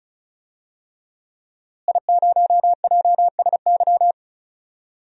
Morse code, heard as a keyed tone, about 34 wpm, 700 Hz. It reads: I0JSY
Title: DXCC-100 Call Signs - 5 Characters - Repeated 3x - 35wpm